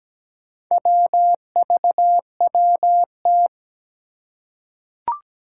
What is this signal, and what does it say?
Morse code, heard as a keyed tone, about 17 wpm, 700 Hz. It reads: WVWT
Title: Four Challenging Letter-Number Combinations 17wpm